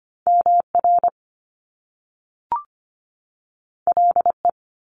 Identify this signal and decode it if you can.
Morse code, heard as a keyed tone, about 25 wpm, 700 Hz. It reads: MR LE
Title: Top Two Letter Combinations in Words 25wpm